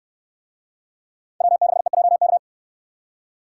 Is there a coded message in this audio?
Yes: Morse code from 1.4 to 2.4 s, heard as a keyed tone, about 67 wpm, 700 Hz. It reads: W6EJK